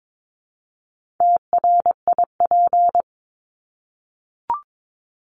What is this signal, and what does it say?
Morse code, heard as a keyed tone, about 22 wpm, 700 Hz. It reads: TRIP
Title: Top 1251-1500 Words 22wpm